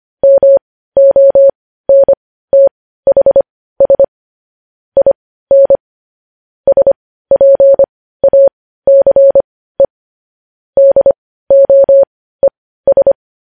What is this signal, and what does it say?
Morse code, heard as a keyed tone, about 25 wpm, spaced as for 15 wpm, 550 Hz. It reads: MONTHS IN SPACE DOES
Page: news headlines in morse code at 15 wpm